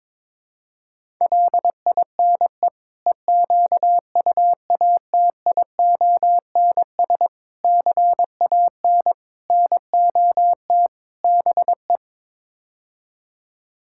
Morse code, heard as a keyed tone, about 22 wpm, 700 Hz. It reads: LINE EQUATIONS CAN NOT BE
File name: practice / 22wpm / Build A Phrase - 5-Word Phrases - 22wpm